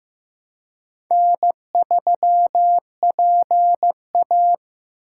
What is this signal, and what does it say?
Morse code, heard as a keyed tone, about 18 wpm, 700 Hz. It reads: N3PA
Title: Speed Racing - USA 1x2 Call Signs 15wpm